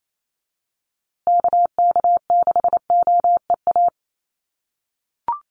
Morse code, heard as a keyed tone, about 28 wpm, 700 Hz. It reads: KK6OEA
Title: USA Call Signs Repeated 2x 28wpm